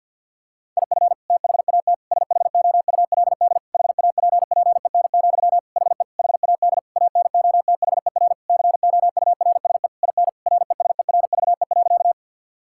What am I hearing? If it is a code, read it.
Morse code, heard as a keyed tone, about 52 wpm, 700 Hz. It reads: IF THAT SHOULD HAPPEN, HE HAD ANOTHER COURSE IN RESERVE.
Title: Sentences from Top 1000 Words - Review Part 6 - 50wpm